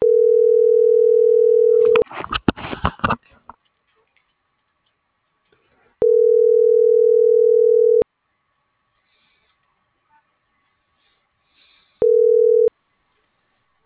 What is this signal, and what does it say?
Ringback tone (North America), 440 + 480 Hz.